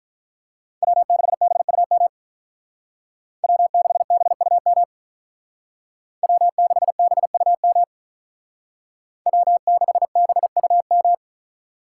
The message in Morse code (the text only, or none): W6BUM W6BUM W6BUM W6BUM